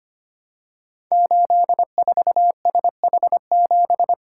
Morse code, heard as a keyed tone, about 25 wpm, 700 Hz. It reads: 84SH7
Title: Code Groups - 5 Letter:Numbers - 25wpm